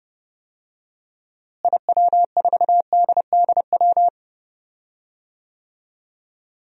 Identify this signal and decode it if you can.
Morse code, heard as a keyed tone, about 29 wpm, 700 Hz. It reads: IW4DDW
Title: DXCC-100 Call Signs - 6 Characters - Repeated 2x - 30wpm